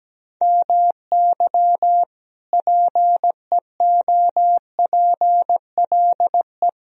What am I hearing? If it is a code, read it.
Morse code, heard as a keyed tone, about 17 wpm, 700 Hz. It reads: MY PEOPLE
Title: Sentences from Top 1000 Words - Review Part 3 - 17wpm